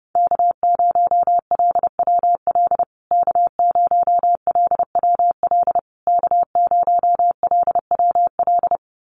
Morse code, heard as a keyed tone, about 30 wpm, 700 Hz. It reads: K0LWL K0LWL K0LWL
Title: USA Call Signs Repeated 3x 30wpm